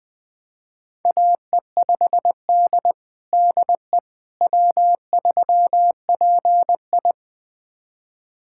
Morse code, heard as a keyed tone, about 20 wpm, 700 Hz. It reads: AE5D DE W3PI
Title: USA Callsign DE Callsign - 4 Characters - 20wpm